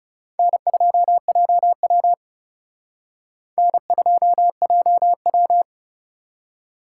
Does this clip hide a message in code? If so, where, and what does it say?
Yes: Morse code from 0.4 to 5.6 s, heard as a keyed tone, about 40 wpm, 700 Hz. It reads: N2JW N2JW